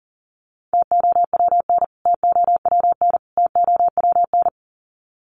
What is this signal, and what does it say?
Morse code, heard as a keyed tone, about 38 wpm, 700 Hz. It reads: TOWN TOWN TOWN
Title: Top 500 Words Repeated 40wpm